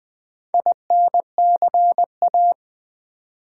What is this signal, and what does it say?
Morse code, heard as a keyed tone, about 20 wpm, 700 Hz. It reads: INCA
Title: Build A Word - 4 Letters At A Time - 11-Letter Words - 20wpm